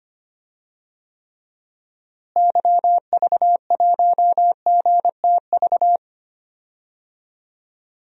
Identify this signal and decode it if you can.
Morse code, heard as a keyed tone, about 25 wpm, 700 Hz. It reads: YV1GTV
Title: DXCC-100 Call Signs - 6 Characters - Repeated 2x - 25wpm